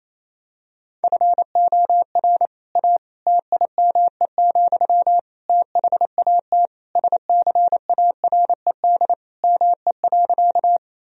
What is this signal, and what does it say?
Morse code, heard as a keyed tone, about 27 wpm, 700 Hz. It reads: FOR A TIME, THAT SCARED ME.
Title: Sentences from Top 1000 Words - Review Part 4 - 28wpm